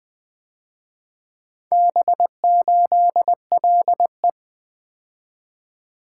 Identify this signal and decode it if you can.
Morse code, heard as a keyed tone, about 24 wpm, 700 Hz. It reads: B8LE
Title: Speed-Racing - Four Letter-Number Combinations - 20wpm